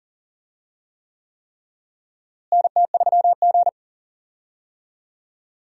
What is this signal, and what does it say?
Morse code, heard as a keyed tone, about 40 wpm, 700 Hz. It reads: NT3G
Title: USA Call Signs - 4 Characters - Repeated 3x - 40wpm